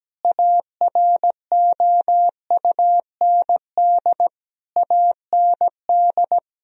Morse code, heard as a keyed tone, about 17 wpm, 700 Hz. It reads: AROUND AND
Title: Sentences from Top 1000 Words - Review Part 3 - 17wpm